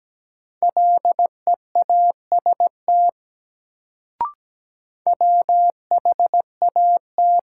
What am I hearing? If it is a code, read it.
Morse code, heard as a keyed tone, about 17 wpm, 700 Hz. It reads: LEAST WHAT
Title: Sets of 2 Words - New Word Drill - 17wpm